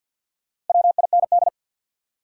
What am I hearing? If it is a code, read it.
Morse code, heard as a keyed tone, about 53 wpm, 700 Hz. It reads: WIND
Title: Sentences from Top 400 Words - New Word Drill - 50wpm